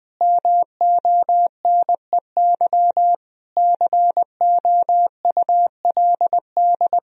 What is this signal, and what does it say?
Morse code, heard as a keyed tone, about 21 wpm, 700 Hz. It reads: MONEY COULD
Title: Sentences from Top 1000 Words - Review Part 4 - 20wpm